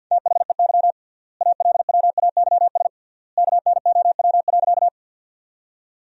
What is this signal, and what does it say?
Morse code, heard as a keyed tone, about 51 wpm, 700 Hz. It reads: THEY ALWAYS KNOW.
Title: 3x Word-Spacing - Sentences from Top 300 Words - 50wpm